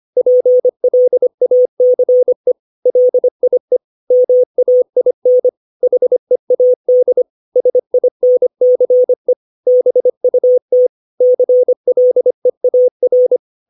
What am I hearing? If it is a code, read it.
Morse code, heard as a keyed tone, about 25 wpm, 500 Hz. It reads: PLACE LIE MAIN HEAD SINCE BUT CLEAR
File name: 632 words / 632 Words 25WPM